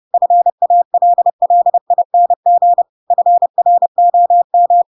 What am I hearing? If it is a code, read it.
Morse code, heard as a keyed tone, about 30 wpm, 700 Hz. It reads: FALLING FROM
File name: Pinocchio / Pinocchio ch7 30 30 ik1hge